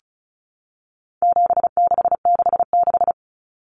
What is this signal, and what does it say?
Morse code, heard as a keyed tone, about 34 wpm, 700 Hz. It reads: 7666
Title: Numbers - 4-Digit 35wpm